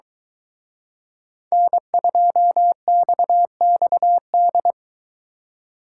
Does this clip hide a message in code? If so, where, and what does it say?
Yes: Morse code from 1.5 to 4.7 s, heard as a keyed tone, about 23 wpm, 700 Hz. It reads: N2XXD